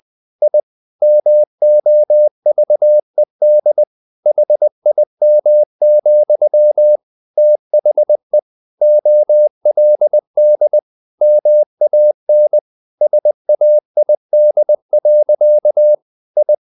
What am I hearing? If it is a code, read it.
Morse code, heard as a keyed tone, about 20 wpm, 600 Hz. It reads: I MOVED HIM, THE OLD MAN SAID. I
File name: the old man and the sea part 2 20wpm